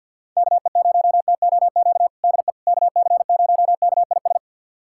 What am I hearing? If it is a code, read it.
Morse code, heard as a keyed tone, about 52 wpm, 700 Hz. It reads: KE0TOQ DE KC0KIS
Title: USA Callsign DE Callsign - 6 Characters - 50wpm